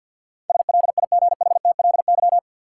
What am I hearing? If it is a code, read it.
Morse code, heard as a keyed tone, about 52 wpm, 700 Hz. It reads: SLIGHTLY
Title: Sentences from Top 1300 Words - New Word Drill - 50wpm